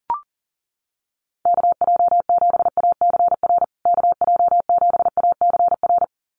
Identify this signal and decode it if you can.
Morse code, heard as a keyed tone, about 40 wpm, 700 Hz. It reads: KJ7ACR KJ7ACR
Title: USA Call Signs Repeated 2x 40wpm